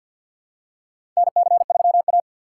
Morse code, heard as a keyed tone, about 47 wpm, 700 Hz. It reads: NC3A
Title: USA Call Signs - 4 Characters - 3x Character Spacing - 50wpm